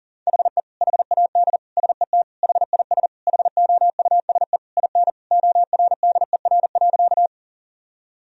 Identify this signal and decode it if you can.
Morse code, heard as a keyed tone, about 40 wpm, 700 Hz. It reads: HE HAD SET HIS HOUSE IN ORDER.
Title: Sentences from Top 1000 Words - Review Part 2 - 40wpm